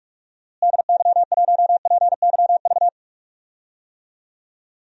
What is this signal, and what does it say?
Morse code, heard as a keyed tone, about 46 wpm, 700 Hz. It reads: DY1PYV
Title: DXCC-100 Call Signs - 6 Characters - Repeated 3x - 45wpm